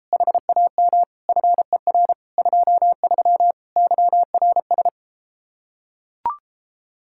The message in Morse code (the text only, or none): HAM FER 23 YRS